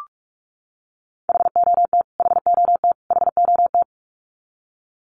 Morse code, heard as a keyed tone, about 44 wpm, 700 Hz. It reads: HOT HOT HOT
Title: Top 500 Words Repeated 45wpm